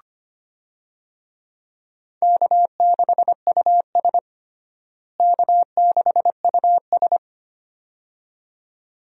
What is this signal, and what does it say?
Morse code, heard as a keyed tone, about 25 wpm, 700 Hz. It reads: K6US K6US
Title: USA Call Signs - 4 Characters - Repeated 2x - 25wpm